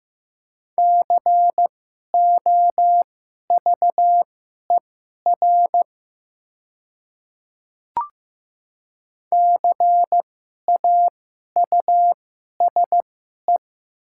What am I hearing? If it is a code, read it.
Morse code, heard as a keyed tone, about 15 wpm, spaced as for 11 wpm, 700 Hz. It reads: COVER CAUSE
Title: 2x Character Spacing - Top 300 Words - 15wpm